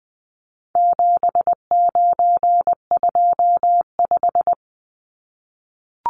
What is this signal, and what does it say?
Morse code, heard as a keyed tone, about 20 wpm, 700 Hz. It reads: 7925
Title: Numbers - 4-Digit 20wpm